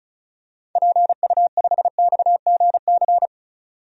Morse code, heard as a keyed tone, about 36 wpm, 700 Hz. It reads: PU5XGC
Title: DXCC-100 Call Signs - 6 Characters - 35wpm